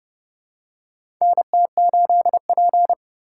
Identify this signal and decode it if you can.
Morse code, heard as a keyed tone, about 30 wpm, 700 Hz. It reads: NT8P
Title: USA Call Signs - 4 Characters - 4x Character Spacing - 30wpm